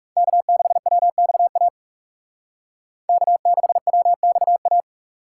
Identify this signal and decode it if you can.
Morse code, heard as a keyed tone, about 38 wpm, 700 Hz. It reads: K6WXA K6WXA